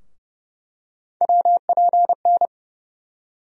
Morse code, heard as a keyed tone, about 30 wpm, 700 Hz. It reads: WPN
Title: USA Call Sign Postfixes - 3 Letters 30wpm